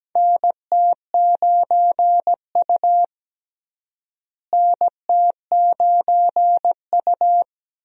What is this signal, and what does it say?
Morse code, heard as a keyed tone, about 17 wpm, 700 Hz. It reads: NT9U NT9U
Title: USA Call Signs - 4 Characters - Repeated 3x - 17wpm